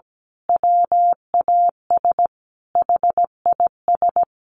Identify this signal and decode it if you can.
Morse code, heard as a keyed tone, about 17 wpm, 700 Hz. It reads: WAS HIS
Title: Sentences from Top 1000 Words 17wpm